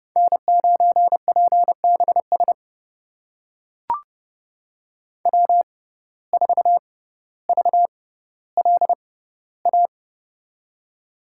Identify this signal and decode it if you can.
Morse code, heard as a keyed tone, about 31 wpm, spaced as for 16 wpm, 700 Hz. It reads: N9PBS W4VLA
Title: USA Call Signs - 5 Characters - 6x Character Spacing - 30wpm